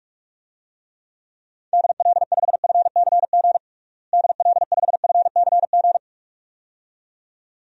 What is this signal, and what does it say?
Morse code, heard as a keyed tone, about 46 wpm, 700 Hz. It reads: DL5FCG DL5FCG
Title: DXCC-100 Call Signs - 6 Characters - Repeated 2x - 45wpm